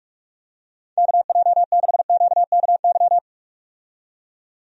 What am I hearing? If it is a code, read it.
Morse code, heard as a keyed tone, about 46 wpm, 700 Hz. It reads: KJ6QKY
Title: USA Call Signs - 6 Characters - 45wpm